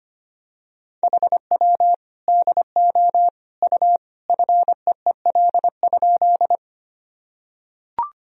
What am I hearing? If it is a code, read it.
Morse code, heard as a keyed tone, about 25 wpm, 700 Hz. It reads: HW DO U FEEL?
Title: Sentences from Top 400 Words and Abbreviations 25wpm